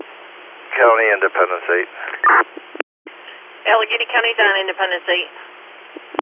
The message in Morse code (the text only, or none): none